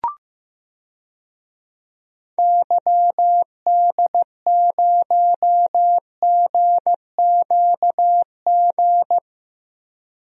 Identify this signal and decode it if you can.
Morse code, heard as a keyed tone, about 15 wpm, 700 Hz. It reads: YD0GQG
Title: DXCC-100 Call Signs - 6 Characters - 15wpm